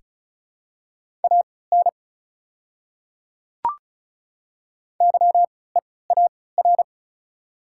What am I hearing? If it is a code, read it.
Morse code, heard as a keyed tone, about 36 wpm, spaced as for 20 wpm, 690 Hz. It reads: AN YEAR
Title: 3x Character Spacing - Top 200 Words - 35wpm